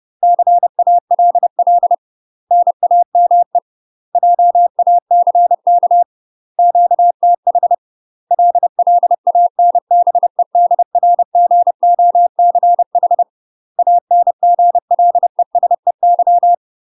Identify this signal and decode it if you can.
Morse code, heard as a keyed tone, about 30 wpm, 700 Hz. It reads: CALL NAME JACK QTH LLANBEDRGOCH ANGLESEY
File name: UK MASTER QSOs TRAINING / 30WPM / UK MASTER QSOs 0000